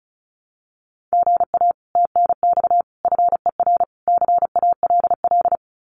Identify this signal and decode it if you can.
Morse code, heard as a keyed tone, about 34 wpm, 700 Hz. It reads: GA TNX FER CALL
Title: Common QSO Phrases Encore 35wpm